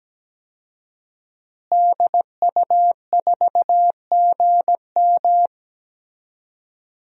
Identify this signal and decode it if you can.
Morse code, heard as a keyed tone, about 17 wpm, 700 Hz. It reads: DU4GM
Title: DXCC-100 Call Signs - 5 Characters - Repeated 3x - 17wpm